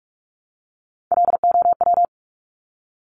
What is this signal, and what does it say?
Morse code, heard as a keyed tone, about 46 wpm, 700 Hz. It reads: LOW
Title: Top Three Letter Combinations in Words 45wpm